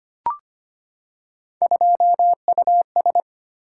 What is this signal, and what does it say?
Morse code, heard as a keyed tone, about 25 wpm, 700 Hz. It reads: 2US
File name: Three Letter-Number Combinations - 25wpm